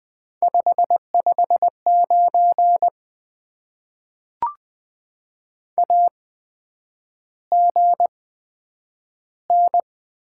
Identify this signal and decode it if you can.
Morse code, heard as a keyed tone, about 20 wpm, spaced as for 9 wpm, 700 Hz. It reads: 559 AGN
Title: Top 90 QSO Elements - 8x Character Spacing - 20wpm